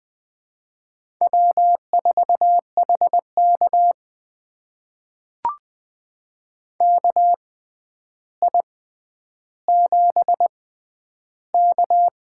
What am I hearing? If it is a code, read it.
Morse code, heard as a keyed tone, about 20 wpm, spaced as for 10 wpm, 700 Hz. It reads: W4HK KI7K
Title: USA Call Signs - 6x Character Spacing - 20wpm